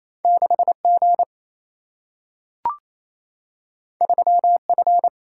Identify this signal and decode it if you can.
Morse code, heard as a keyed tone, about 28 wpm, 700 Hz. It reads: 6G 3F
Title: Single-Letter Single-Number 28wpm